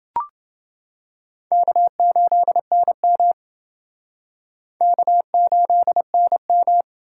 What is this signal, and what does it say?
Morse code, heard as a keyed tone, about 28 wpm, 700 Hz. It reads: K8NM K8NM